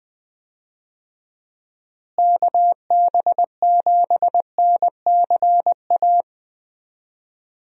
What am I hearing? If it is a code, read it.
Morse code, heard as a keyed tone, about 21 wpm, 700 Hz. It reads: KB7NCA
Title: USA Call Signs - 6 Characters - Repeated 2x - 20wpm